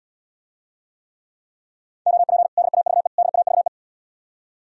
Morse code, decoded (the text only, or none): OP DAVE DAVE